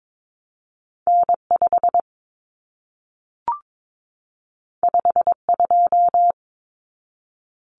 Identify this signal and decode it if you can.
Morse code, heard as a keyed tone, about 22 wpm, 700 Hz. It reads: N5 52
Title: Common Cut Numbers - 2-Digit 22wpm